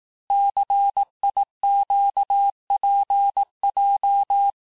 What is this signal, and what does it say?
Morse code, heard as a keyed tone, about 18 wpm, 800 Hz. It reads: CIQPJ
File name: cours-du-lundi files / cours-1588-texte-chiffre-vitesse-17MPM